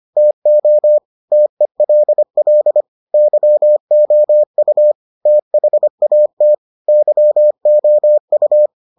Morse code, heard as a keyed tone, about 25 wpm, 600 Hz. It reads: TO TELL YOU THAT YOU